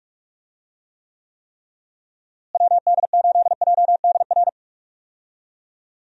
Morse code, heard as a keyed tone, about 46 wpm, 700 Hz. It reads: WD8JDR
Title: USA Call Signs - 6 Characters - Repeated 3x - 45wpm